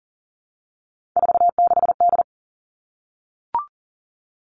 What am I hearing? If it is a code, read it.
Morse code, heard as a keyed tone, about 40 wpm, 700 Hz. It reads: 46D
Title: All Cut Numbers - 3-Digit 40wpm